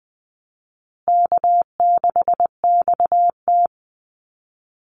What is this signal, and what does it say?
Morse code, heard as a keyed tone, about 20 wpm, 700 Hz. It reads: K6XT
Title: Easy Call Signs Repeated 2x - 20wpm